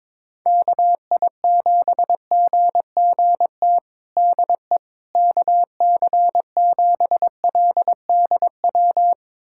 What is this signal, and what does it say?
Morse code, heard as a keyed tone, about 22 wpm, 700 Hz. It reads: KI7GGT DE KC7LDW
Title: USA Callsign DE Callsign - 6 Characters - 22wpm